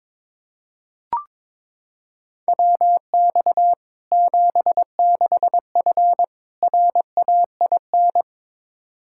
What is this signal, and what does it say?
Morse code, heard as a keyed tone, about 21 wpm, 700 Hz. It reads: WX 76F RAIN
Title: Practice QSO - Exchange 2 - Info - 22wpm